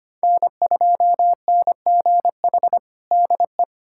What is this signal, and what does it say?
Morse code, heard as a keyed tone, about 25 wpm, 700 Hz. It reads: N2NGH DE
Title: USA Callsign DE Callsign 25wpm